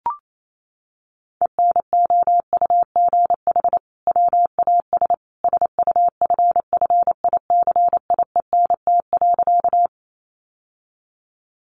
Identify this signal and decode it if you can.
Morse code, heard as a keyed tone, about 28 wpm, 700 Hz. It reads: ENOUGH WAS SUFFICIENT.
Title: Sentences from Top 1400 Words 28wpm